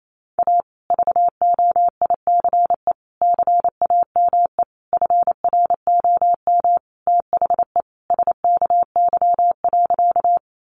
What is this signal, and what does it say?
Morse code, heard as a keyed tone, about 28 wpm, 700 Hz. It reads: A VOICE CAME FROM THE SKY.
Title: Sentences from Top 1200 Words 28wpm